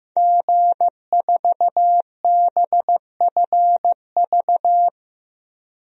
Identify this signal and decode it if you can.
Morse code, heard as a keyed tone, about 15 wpm, 700 Hz. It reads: G4BFV
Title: DXCC-100 Call Signs - 5 Characters - Repeated 2x - 15wpm